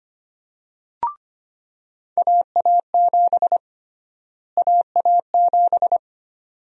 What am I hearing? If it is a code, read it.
Morse code, heard as a keyed tone, about 25 wpm, 700 Hz. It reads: AA7 AA7
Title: USA Call Sign Prefixes - Repeated 2x - 25wpm